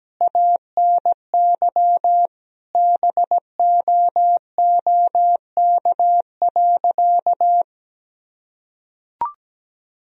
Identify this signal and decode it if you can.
Morse code, heard as a keyed tone, about 17 wpm, 700 Hz. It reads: ANY BOOK.
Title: Sentences from Top 1000 Words - Review Part 1 - 17wpm